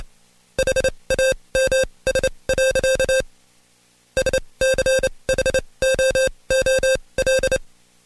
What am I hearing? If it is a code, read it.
Morse code, heard as a keyed tone, about 28 wpm, spaced as for 22 wpm, 530 Hz. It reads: HAMS. SCHOOL
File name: Code Quick / AudioDownld2 / Seg137-Stepper10 21WPM